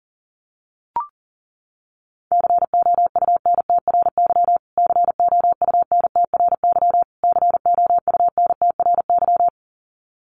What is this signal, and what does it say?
Morse code, heard as a keyed tone, about 40 wpm, 700 Hz. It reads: COUNTRY COUNTRY COUNTRY
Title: Top 500 Words Repeated 40wpm